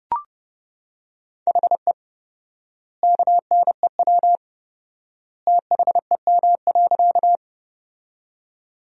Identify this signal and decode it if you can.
Morse code, heard as a keyed tone, about 32 wpm, 700 Hz. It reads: HE KNEW THEM.
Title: 4x Word-Spacing - Sentences from Top 400 Words - 30wpm